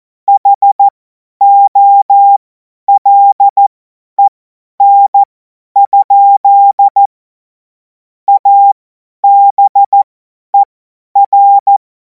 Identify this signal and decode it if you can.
Morse code, heard as a keyed tone, about 14 wpm, spaced as for 10 wpm, 800 Hz. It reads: HOLEN? ABER
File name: Morsekurs / Witze / Witz 17. März 2026 50 F70